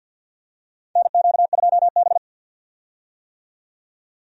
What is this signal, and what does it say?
Morse code, heard as a keyed tone, about 52 wpm, 700 Hz. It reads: NQ2B
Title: USA Call Signs - 4 Characters - 50wpm